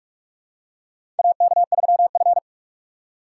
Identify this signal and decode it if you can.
Morse code, heard as a keyed tone, about 42 wpm, 700 Hz. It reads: AK3F
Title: USA Call Signs - 4 Characters - 3x Character Spacing - 45wpm